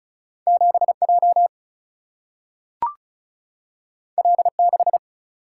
Morse code, heard as a keyed tone, about 36 wpm, 700 Hz. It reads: 7 J L 6
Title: Single-Letter Single-Number 35wpm